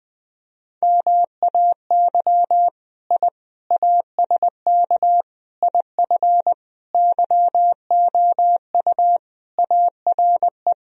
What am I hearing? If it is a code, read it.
Morse code, heard as a keyed tone, about 21 wpm, 700 Hz. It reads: MAY I ASK IF YOU ARE
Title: Sentences from Top 1000 Words - Review Part 3 - 20wpm